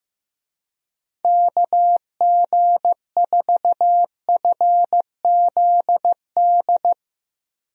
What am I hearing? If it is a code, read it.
Morse code, heard as a keyed tone, about 16 wpm, 700 Hz. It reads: KG4FZD
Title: USA Call Signs - 6 Characters - Repeated 2x - 15wpm